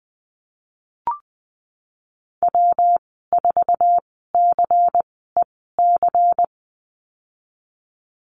Morse code, heard as a keyed tone, about 20 wpm, spaced as for 14 wpm, 700 Hz. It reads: W4CEC
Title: 179 Lesson 29g - Slash - Call Signs 20wpm